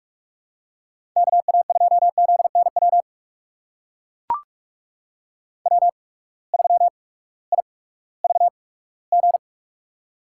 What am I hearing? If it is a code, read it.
Morse code, heard as a keyed tone, about 44 wpm, spaced as for 20 wpm, 700 Hz. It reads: KA2ZNW W3IVG
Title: USA Call Signs - 8x Character Spacing - 45wpm